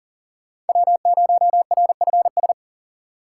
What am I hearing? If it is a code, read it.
Morse code, heard as a keyed tone, about 40 wpm, 700 Hz. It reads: W0RFS
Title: USA Call Signs - 3x Character Spacing - 40wpm